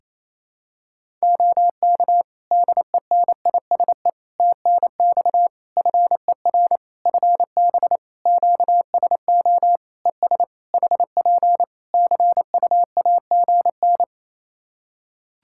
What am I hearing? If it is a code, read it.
Morse code, heard as a keyed tone, about 27 wpm, 700 Hz. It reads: OK DENISE TNX FER FB QSO ES HP CUAGN
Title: K1USN SST QSO Practice 28wpm